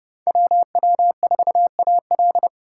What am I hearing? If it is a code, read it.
Morse code, heard as a keyed tone, about 31 wpm, 700 Hz. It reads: WW4AL